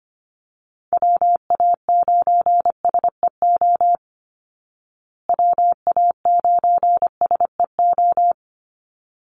Morse code, heard as a keyed tone, about 25 wpm, 700 Hz. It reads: WA9SEO WA9SEO
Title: Call Signs Repeated 3x - 25wpm